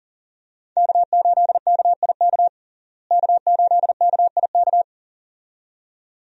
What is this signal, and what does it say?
Morse code, heard as a keyed tone, about 40 wpm, 700 Hz. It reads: K8KIK K8KIK
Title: USA Call Signs - 5 Characters - Repeated 2x - 40wpm